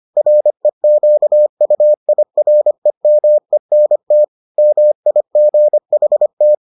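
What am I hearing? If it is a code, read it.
Morse code, heard as a keyed tone, about 25 wpm, 600 Hz. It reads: REQUIREMENT MIGHT
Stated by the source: Morse edition of the Foundations of Amateur Radio episodes.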